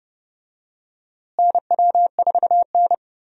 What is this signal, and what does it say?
Morse code, heard as a keyed tone, about 29 wpm, 700 Hz. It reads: NW4N
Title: USA Call Signs - 4 Characters - Repeated 2x - 30wpm